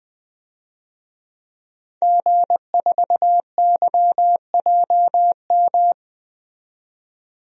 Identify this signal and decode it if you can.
Morse code, heard as a keyed tone, about 20 wpm, 700 Hz. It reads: G4YJM
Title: DXCC-100 Call Signs - 5 Characters - 20wpm